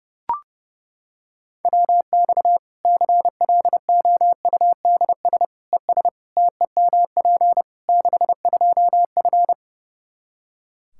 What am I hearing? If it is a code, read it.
Morse code, heard as a keyed tone, about 30 wpm, 700 Hz. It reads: WX CLOUDS ES TEMP 62F
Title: K1USN SST QSO Practice 30wpm